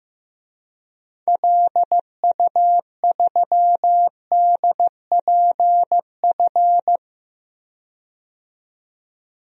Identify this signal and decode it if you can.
Morse code, heard as a keyed tone, about 15 wpm, 700 Hz. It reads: LU3DPF
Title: DXCC-100 Call Signs - 6 Characters - 15wpm